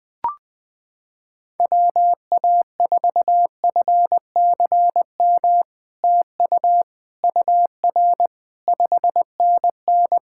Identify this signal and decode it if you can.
Morse code, heard as a keyed tone, about 20 wpm, 700 Hz. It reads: WA4FCM TU UR 5NN
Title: SOTA QSO Practice 20wpm